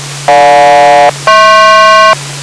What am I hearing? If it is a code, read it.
Aircraft SELCAL, heard as two pairs of tones. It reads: GJ-HP